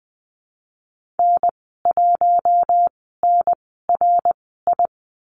Morse code, heard as a keyed tone, about 20 wpm, spaced as for 15 wpm, 700 Hz. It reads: N1NRI
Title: 055 Lesson 10g - Letter r - Call Signs 20wpm